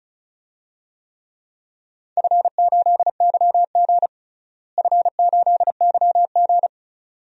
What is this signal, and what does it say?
Morse code, heard as a keyed tone, about 36 wpm, 700 Hz. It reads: F8YG F8YG
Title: DXCC-100 Call Signs - 4 Characters - Repeated 2x - 35wpm